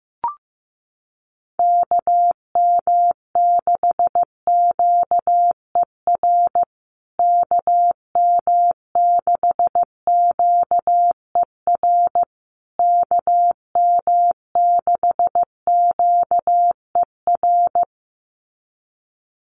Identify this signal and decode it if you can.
Morse code, heard as a keyed tone, about 15 wpm, 700 Hz. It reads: KM6QER KM6QER KM6QER
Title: USA Call Signs Repeated 3x 15wpm